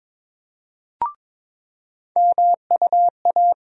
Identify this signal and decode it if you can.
Morse code, heard as a keyed tone, about 21 wpm, 700 Hz. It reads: MUA
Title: USA Call Sign Postfixes - 3 Letters 22wpm